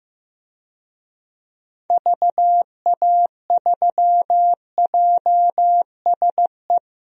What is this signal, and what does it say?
Morse code, heard as a keyed tone, about 15 wpm, 700 Hz. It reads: VA3JSE
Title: DXCC-100 Call Signs - 6 Characters - Repeated 3x - 15wpm